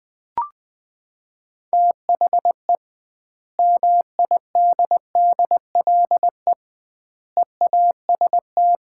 Morse code, heard as a keyed tone, about 20 wpm, 700 Hz. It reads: THE MIDDLE EAST
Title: 2x Word-Spacing - Sets of 3 Words - 20wpm